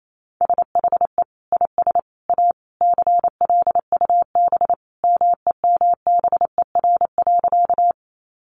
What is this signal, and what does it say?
Morse code, heard as a keyed tone, about 28 wpm, 700 Hz. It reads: SHE IS A CLUB MEMBER.
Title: Sentences from Top 1200 Words 28wpm